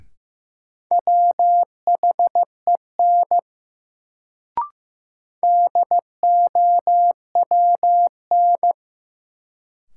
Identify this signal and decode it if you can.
Morse code, heard as a keyed tone, about 15 wpm, 700 Hz. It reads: WHEN DOWN
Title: Sets of 2 Words - New Word Drill - 15wpm